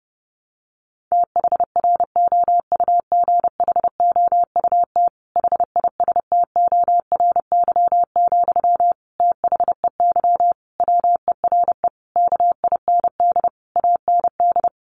Morse code, heard as a keyed tone, about 30 wpm, 700 Hz. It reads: THROUGHOUT HISTORY, THEY WERE KIND AND
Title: Sentences from Top 800 Words 30wpm